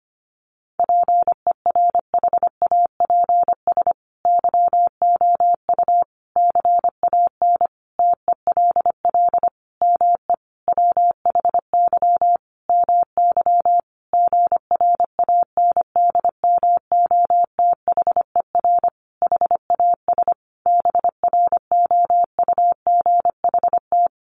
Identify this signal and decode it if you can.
Morse code, heard as a keyed tone, about 25 wpm, 700 Hz. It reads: PERHAPS YOU CAN TELL ME WHY MY GRANDMOTHER HAS BROUGHT
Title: Sentences from Top 1200 Words 25wpm